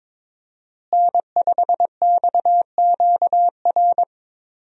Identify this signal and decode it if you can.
Morse code, heard as a keyed tone, about 22 wpm, 700 Hz. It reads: N5XQR
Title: USA Call Signs - 5 Characters - Repeated 2x - 22wpm